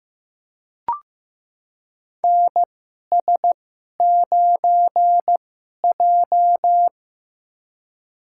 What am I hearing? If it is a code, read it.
Morse code, heard as a keyed tone, about 15 wpm, spaced as for 11 wpm, 700 Hz. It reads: NS9J
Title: USA Call Signs - 4 Characters - 2x Character Spacing - 15wpm